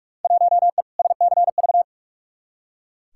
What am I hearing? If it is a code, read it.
Morse code, heard as a keyed tone, about 43 wpm, 700 Hz. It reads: 1E SCV
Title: Field Day Exchanges - 45wpm